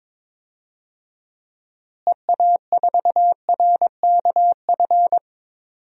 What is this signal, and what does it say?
Morse code, heard as a keyed tone, about 22 wpm, 700 Hz. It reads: EA4RKF
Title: DXCC-100 Call Signs - 6 Characters - Repeated 3x - 22wpm